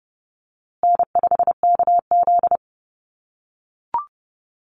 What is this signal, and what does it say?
Morse code, heard as a keyed tone, about 30 wpm, 700 Hz. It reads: N5KZ
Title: USA Call Signs Repeated 3x 30wpm